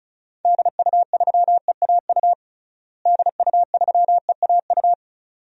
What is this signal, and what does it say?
Morse code, heard as a keyed tone, about 34 wpm, 700 Hz. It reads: DU3EAU DU3EAU
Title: DXCC-100 Call Signs - 6 Characters - Repeated 2x - 35wpm